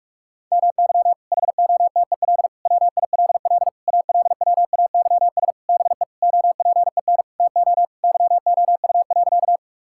Morse code, heard as a keyed tone, about 47 wpm, 700 Hz. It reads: MY HOTEL WILL ALWAYS BE OPEN TO YOU.
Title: Sentences from Top 1600 Words 45wpm